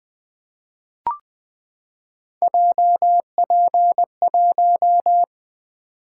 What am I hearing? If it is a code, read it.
Morse code, heard as a keyed tone, about 20 wpm, 700 Hz. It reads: JP1
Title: Three Letter-Number Combinations - 20wpm